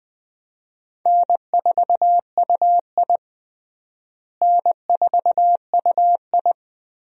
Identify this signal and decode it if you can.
Morse code, heard as a keyed tone, about 20 wpm, 700 Hz. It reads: N4UI N4UI
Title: USA Call Signs - 4 Characters - Repeated 2x - 20wpm